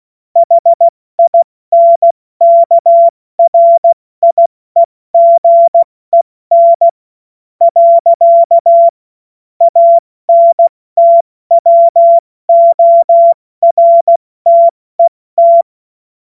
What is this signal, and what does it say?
Morse code, heard as a keyed tone, about 16 wpm, 680 Hz. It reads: HINKRIEGEN . ANTWORTET
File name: Morsekurs / Witze / Witz 20. Juni 2023 70 F80